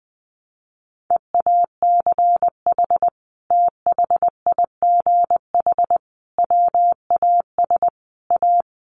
Morse code, heard as a keyed tone, about 20 wpm, 700 Hz. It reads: EACH THIGH WAS A
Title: A Sound Of Thunder 20wpm